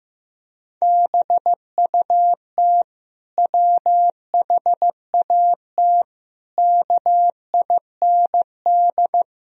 Morse code, heard as a keyed tone, about 15 wpm, 700 Hz. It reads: BUT WHAT KIND
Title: Sentences from Top 1000 Words - Review Part 1 - 15wpm